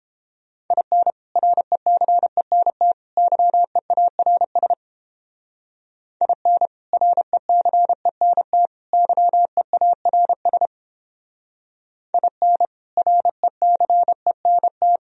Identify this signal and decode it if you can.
Morse code, heard as a keyed tone, about 31 wpm, 700 Hz. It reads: IN RECENT YEARS IN RECENT YEARS IN RECENT